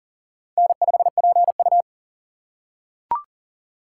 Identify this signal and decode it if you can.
Morse code, heard as a keyed tone, about 40 wpm, 700 Hz. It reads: N5PU
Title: K1USN SST QSO Practice - 40wpm